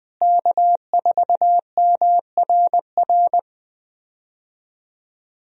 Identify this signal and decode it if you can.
Morse code, heard as a keyed tone, about 20 wpm, 700 Hz. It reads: K4MRR
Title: USA Call Signs - 5 Characters - Repeated 3x - 20wpm